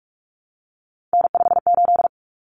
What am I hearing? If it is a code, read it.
Morse code, heard as a keyed tone, about 44 wpm, 700 Hz. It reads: N58
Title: Common Cut Numbers - 3-Digit 45wpm